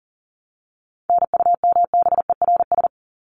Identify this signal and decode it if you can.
Morse code, heard as a keyed tone, about 40 wpm, 700 Hz. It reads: NUMBERS
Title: ES:S Suffix - Top 100 Words with Base Words 40wpm